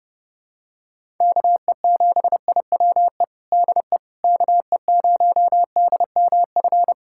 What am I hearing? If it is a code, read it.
Morse code, heard as a keyed tone, about 32 wpm, 700 Hz. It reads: KE7IWE DE KE0DMF
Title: USA Callsign DE Callsign 30wpm